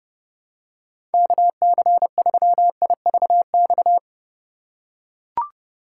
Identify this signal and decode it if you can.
Morse code, heard as a keyed tone, about 30 wpm, 700 Hz. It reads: KC3IVX
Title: USA Call Signs - 8x Character Spacing - 30wpm